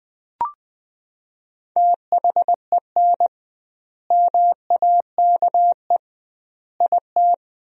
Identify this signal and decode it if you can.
Morse code, heard as a keyed tone, about 20 wpm, 700 Hz. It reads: THEN MAKE IT
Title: 2x Word-Spacing - Sentences from Top 100 Words - 20wpm